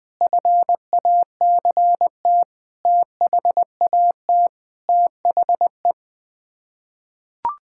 Morse code, heard as a keyed tone, about 20 wpm, 700 Hz. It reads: FACT THAT THE
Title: Sets of 8 Words - 20wpm